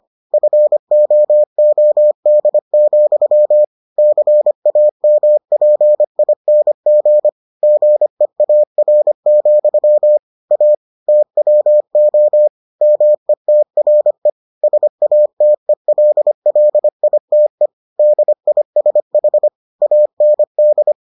Morse code, heard as a keyed tone, about 25 wpm, 600 Hz. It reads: FOOD, CAMPING GEAR, A TWO METRE SATELLITE DISH AND